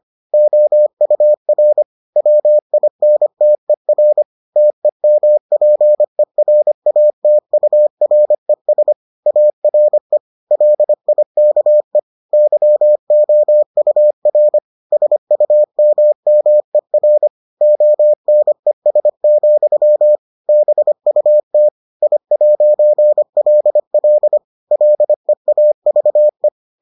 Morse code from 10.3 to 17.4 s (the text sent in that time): LIKE YOUR SUMMER